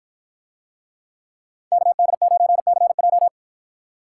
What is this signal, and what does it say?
Morse code, heard as a keyed tone, about 53 wpm, 700 Hz. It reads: KD9CJ